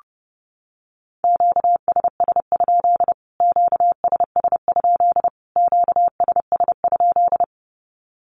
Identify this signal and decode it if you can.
Morse code, heard as a keyed tone, about 30 wpm, 700 Hz. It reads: QSS? QSS? QSS?
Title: Q-Codes Repeated 3x 30wpm